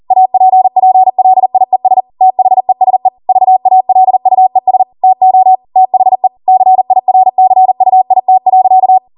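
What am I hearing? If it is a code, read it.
Morse code, heard as a keyed tone, about 40 wpm, 750 Hz. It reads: APPLIES THESE VALUES TO THE CIRCUIT.
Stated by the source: Code practice at 40 wpm.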